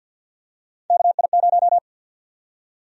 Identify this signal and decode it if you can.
Morse code, heard as a keyed tone, about 51 wpm, 700 Hz. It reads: KI0
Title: USA Call Sign Prefixes - 50wpm